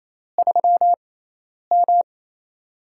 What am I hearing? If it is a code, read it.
Morse code, heard as a keyed tone, about 29 wpm, 700 Hz. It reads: 3 M
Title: USA Call Signs - 4 Characters - 6x Character Spacing - 28wpm